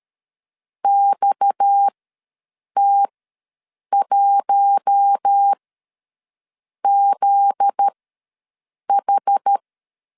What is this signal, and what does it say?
Morse code, heard as a keyed tone, about 13 wpm, spaced as for 6 wpm, 800 Hz. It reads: XT1ZH
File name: LEZIONI / 7 GF9Z / 7 GF9Z ws 12wpm 3321